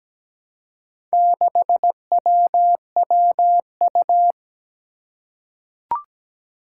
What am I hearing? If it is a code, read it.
Morse code, heard as a keyed tone, about 17 wpm, 700 Hz. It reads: 6WWU
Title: Four Challenging Letter-Number Combinations 17wpm